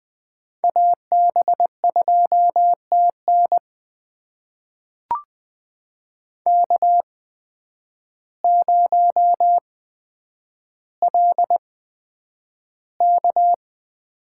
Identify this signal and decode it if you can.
Morse code, heard as a keyed tone, about 20 wpm, spaced as for 9 wpm, 700 Hz. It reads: AB2TN K0LK
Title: USA Call Signs - 8x Character Spacing - 20wpm